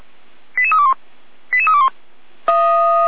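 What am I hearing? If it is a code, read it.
Five-tone selective call (ZVEI), heard as quick runs of five tones, tones of about 80 ms. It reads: 88321 88321
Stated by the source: ZVEI five-tone selective call 88191 received via SDR.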